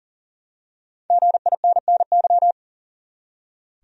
Morse code, heard as a keyed tone, about 40 wpm, 700 Hz. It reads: GINNY
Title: USA Common Names - Max 5 Letters - 40wpm